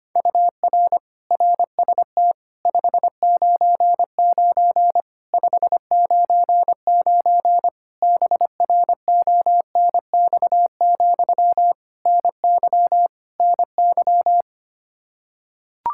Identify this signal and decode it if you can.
Morse code, heard as a keyed tone, about 25 wpm, 700 Hz. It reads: UR RST 599 599 BRONX, NY NY
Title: Practice QSO - Exchange 1 - Beginning - 25wpm